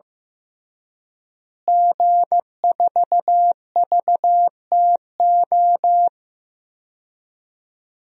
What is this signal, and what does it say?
Morse code, heard as a keyed tone, about 15 wpm, 700 Hz. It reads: G4VTO
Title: DXCC-100 Call Signs - 5 Characters - Repeated 3x - 15wpm